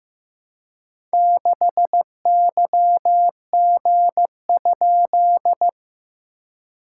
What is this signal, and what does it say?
Morse code, heard as a keyed tone, about 15 wpm, 700 Hz. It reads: 6YG?
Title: SOTA QSO Practice 15wpm